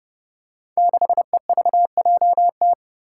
Morse code, heard as a keyed tone, about 30 wpm, 700 Hz. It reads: 6EVJT
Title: Five Challenging Letter-Number Combinations 30wpm